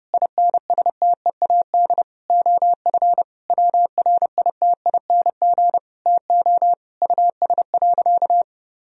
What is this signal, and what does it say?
Morse code, heard as a keyed tone, about 30 wpm, 700 Hz. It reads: INSTEAD OF WRITING TO US.
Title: Sentences from Top 1000 Words - Review Part 4 - 30wpm